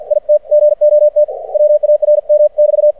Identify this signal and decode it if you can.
Morse code, heard as a keyed tone, about 53 wpm, 600 Hz. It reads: IT GOT WARM<BT>
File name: archived / CRSnet